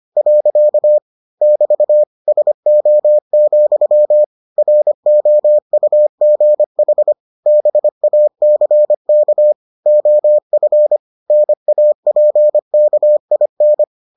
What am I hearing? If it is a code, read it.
Morse code, heard as a keyed tone, about 25 wpm, 600 Hz. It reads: . <BT> SO, ROUGH BACK OF NAPKIN